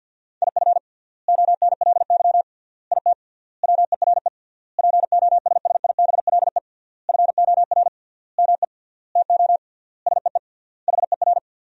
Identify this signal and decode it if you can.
Morse code, heard as a keyed tone, about 52 wpm, 700 Hz. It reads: IF ONLY IT WERE POSSIBLE FOR ME TO SEE HER
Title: 3x Word-Spacing - Sentences from Top 500 Words - 50wpm